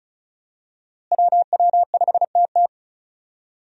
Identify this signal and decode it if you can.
Morse code, heard as a keyed tone, about 36 wpm, 700 Hz. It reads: WW5TT
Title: USA Call Signs - 5 Characters - Repeated 3x - 35wpm